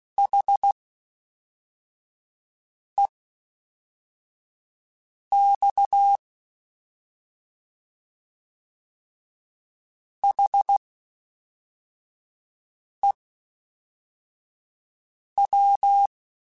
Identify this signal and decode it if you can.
Morse code, heard as a keyed tone, about 16 wpm, spaced as for 4 wpm, 800 Hz. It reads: HEX HEW
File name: H Prefixes / H mp3s